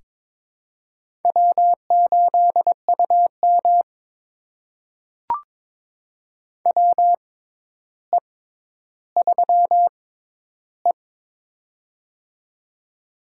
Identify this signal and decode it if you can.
Morse code, heard as a keyed tone, about 22 wpm, spaced as for 11 wpm, 700 Hz. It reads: W8UM WE3E
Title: USA Call Signs - 4 Characters - 6x Character Spacing - 22wpm